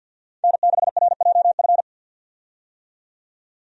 Morse code, heard as a keyed tone, about 50 wpm, 700 Hz. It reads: N6RJF
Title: USA Call Signs - 5 Characters - 50wpm